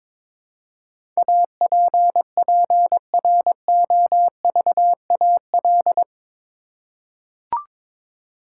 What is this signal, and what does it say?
Morse code, heard as a keyed tone, about 22 wpm, 700 Hz. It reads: APPROVAL
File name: practice / 22wpm / Eight Letter Words - 22wpm